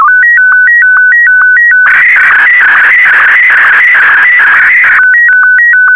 SSTV (slow-scan television), Martin 1.
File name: sstv / M1-VK4MSL